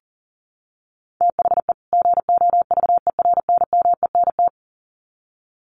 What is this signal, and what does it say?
Morse code, heard as a keyed tone, about 40 wpm, 700 Hz. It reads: THE GOVERNMENT
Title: Sets of 2 Words 40wpm